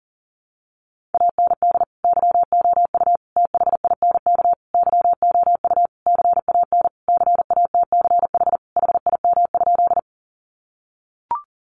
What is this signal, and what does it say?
Morse code, heard as a keyed tone, about 40 wpm, 700 Hz. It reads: AND YOU THINK YOU CAN CATCH HIM?
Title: Sentences from Top 1500 Words 40wpm